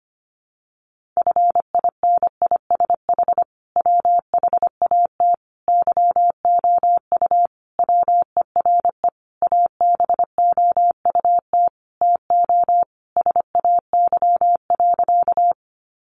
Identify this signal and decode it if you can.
Morse code, heard as a keyed tone, about 25 wpm, 700 Hz. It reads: FINISH WHAT YOU WERE ABOUT TO SAY.
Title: Sentences from Top 1200 Words 25wpm